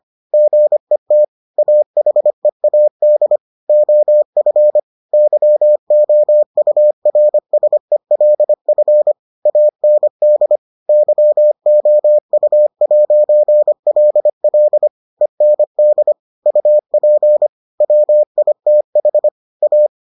Morse code, heard as a keyed tone, about 25 wpm, 600 Hz. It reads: GET AHEAD OF YOURSELF AND YOU'LL END UP WITH A